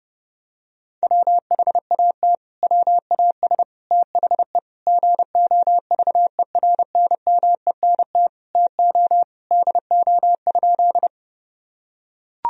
Morse code, heard as a keyed tone, about 30 wpm, 700 Hz. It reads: WHAT WAS THE GOVERNMENT TO DO?
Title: Sentences from Top 1000 Words - Review Part 3 - 30wpm